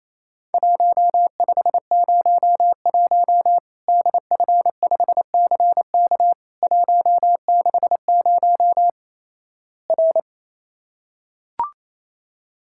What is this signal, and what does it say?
Morse code, heard as a keyed tone, about 27 wpm, 700 Hz. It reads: 1501 DF5CK 160 R
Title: Worked All Europe DX Contest - QTC Exchange - 28wpm